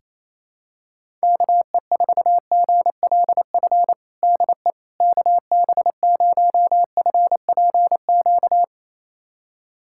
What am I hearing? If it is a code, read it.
Morse code, heard as a keyed tone, about 28 wpm, 700 Hz. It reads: KE4GLF DE KB0FPQ
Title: USA Callsign DE Callsign 28wpm